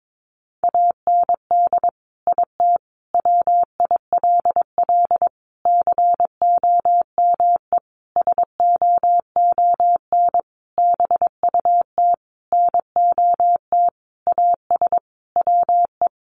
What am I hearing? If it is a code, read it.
Morse code, heard as a keyed tone, about 22 wpm, 700 Hz. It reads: AND IT WILL COME SOON BUT NOT AS WE
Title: Sentences from Top 1100 Words 22wpm